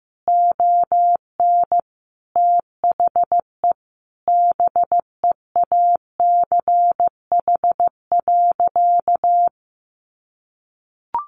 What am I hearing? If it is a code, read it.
Morse code, heard as a keyed tone, about 15 wpm, 700 Hz. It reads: ON THE BEACH.
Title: Sentences from Top 1500 Words 15wpm